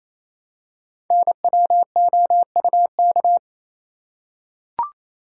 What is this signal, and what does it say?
Morse code, heard as a keyed tone, about 27 wpm, 700 Hz. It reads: NWOUK
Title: Code Groups - 5 Letters - 28wpm